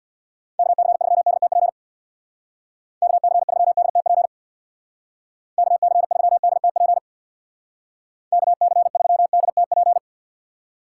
Morse code, heard as a keyed tone, about 65 wpm, 700 Hz. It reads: KC3DTP KC3DTP KC3DTP KC3DTP